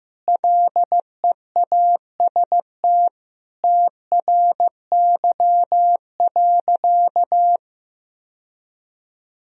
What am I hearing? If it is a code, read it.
Morse code, heard as a keyed tone, about 15 wpm, 700 Hz. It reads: LEAST TRY.
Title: Sentences from Top 1000 Words - Review Part 1 - 15wpm